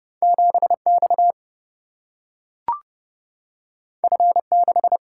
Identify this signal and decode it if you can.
Morse code, heard as a keyed tone, about 29 wpm, 700 Hz. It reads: 7 X F 6
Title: Single-Letter Single-Number 30wpm